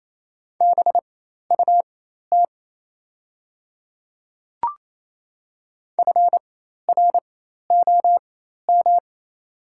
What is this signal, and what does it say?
Morse code, heard as a keyed tone, about 27 wpm, spaced as for 13 wpm, 700 Hz. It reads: BUT FROM
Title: 4x Character Spacing - Top 100 Words - 28wpm